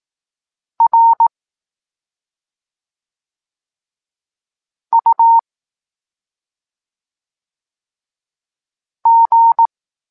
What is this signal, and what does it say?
Morse code, heard as a keyed tone, about 18 wpm, 920 Hz. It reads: R U G
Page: Tekst 23 – Doskonalenie odbioru poznanych liter